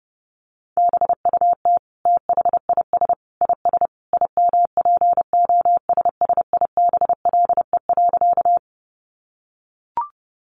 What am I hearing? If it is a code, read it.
Morse code, heard as a keyed tone, about 30 wpm, 700 Hz. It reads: BUT THIS IS IMPOSSIBLE.
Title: Sentences from Top 1100 Words 30wpm